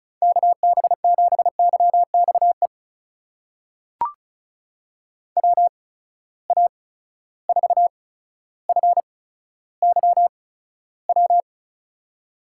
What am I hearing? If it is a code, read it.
Morse code, heard as a keyed tone, about 36 wpm, spaced as for 15 wpm, 700 Hz. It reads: KB7YXE WA4FYW
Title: USA Call Signs - 6 Characters - 8x Character Spacing - 35wpm